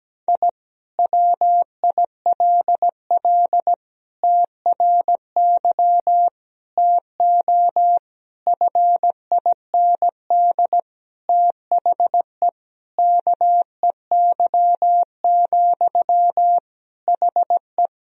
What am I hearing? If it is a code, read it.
Morse code, heard as a keyed tone, about 17 wpm, 700 Hz. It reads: I WILL TRY TO FIND THE KEY, HE
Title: Sentences from Top 1000 Words - Review Part 4 - 17wpm